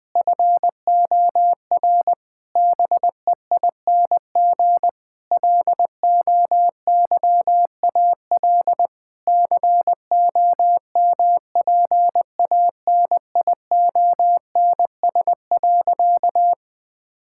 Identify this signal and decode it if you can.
Morse code, heard as a keyed tone, about 20 wpm, 700 Hz. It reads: FOR BEING LOYAL COMPANIONS.
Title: Difficult Sentences with Hints - 20wpm